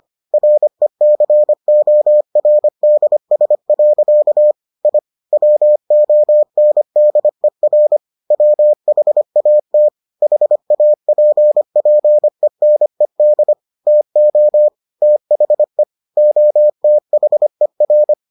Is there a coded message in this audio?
Yes: Morse code from 0.3 to 18.1 s, heard as a keyed tone, about 25 wpm, 600 Hz. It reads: RECORDS. I WONDER WHAT HAPPENED TO THE OTHER